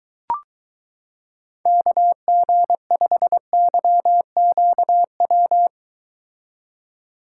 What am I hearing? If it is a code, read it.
Morse code, heard as a keyed tone, about 23 wpm, 700 Hz. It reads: KG5YQW